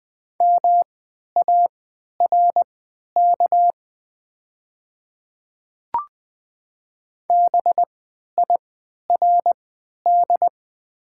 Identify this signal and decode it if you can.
Morse code, heard as a keyed tone, about 20 wpm, spaced as for 11 wpm, 700 Hz. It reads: MARK BIRD
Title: 3x Character Spacing - Top 500 Words - 20wpm